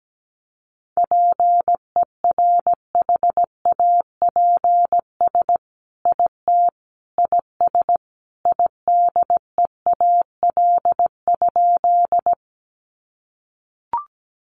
Morse code, heard as a keyed tone, about 17 wpm, 700 Hz. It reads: PERHAPS IT IS IDEAL?
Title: Sentences from Top 1400 Words 17wpm